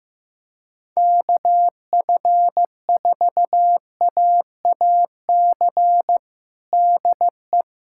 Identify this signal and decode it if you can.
Morse code, heard as a keyed tone, about 15 wpm, 700 Hz. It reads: KF4AAC DE
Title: USA Callsign DE Callsign 15wpm